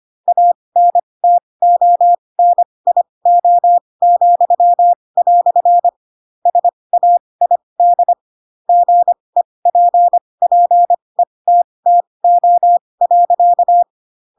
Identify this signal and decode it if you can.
Morse code, heard as a keyed tone, about 25 wpm, spaced as for 20 wpm, 700 Hz. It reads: ANTONIO," SAID GEPPETTO.
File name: Pinocchio / Pinocchio ch2 15 25 ik1hge